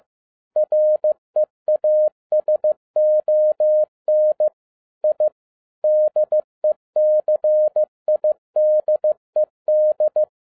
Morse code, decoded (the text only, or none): REASON I DECIDED